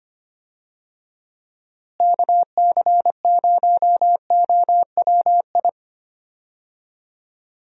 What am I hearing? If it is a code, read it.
Morse code, heard as a keyed tone, about 25 wpm, 700 Hz. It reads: KC0OWI
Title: USA Call Signs - 6 Characters - Repeated 3x - 25wpm